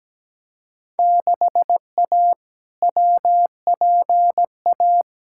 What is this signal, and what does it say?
Morse code, heard as a keyed tone, about 17 wpm, 700 Hz. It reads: 6A WPA
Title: Field Day Exchanges Encore - 17wpm